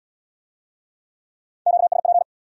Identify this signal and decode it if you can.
Morse code, heard as a keyed tone, about 64 wpm, 700 Hz. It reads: QSP